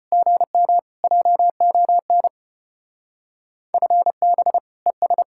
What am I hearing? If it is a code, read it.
Morse code, heard as a keyed tone, about 29 wpm, 700 Hz. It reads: GM JON FB ES
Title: Speed-Racing Common QSO Phrases - 25wpm